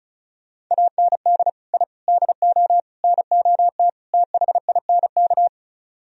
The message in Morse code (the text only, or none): AND I DO NOT THINK